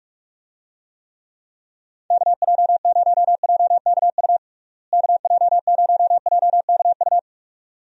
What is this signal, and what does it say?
Morse code, heard as a keyed tone, about 46 wpm, 700 Hz. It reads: KJ0JKU KJ0JKU
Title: USA Call Signs - 6 Characters - Repeated 2x - 45wpm